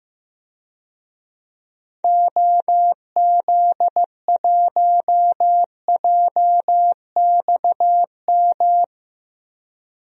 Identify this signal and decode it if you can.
Morse code, heard as a keyed tone, about 15 wpm, 700 Hz. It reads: OZ1JXM
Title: DXCC-100 Call Signs - 6 Characters - Repeated 2x - 15wpm